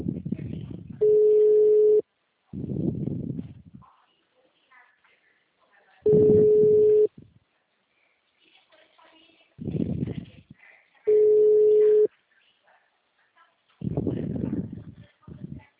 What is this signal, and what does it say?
Ringback tone (Europe, 425 Hz), with speech.
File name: out- -2019- - - .346